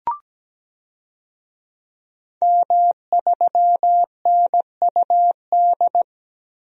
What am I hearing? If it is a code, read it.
Morse code, heard as a keyed tone, about 17 wpm, 700 Hz. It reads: M3NUD
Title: DXCC-100 Call Signs - 5 Characters - 17wpm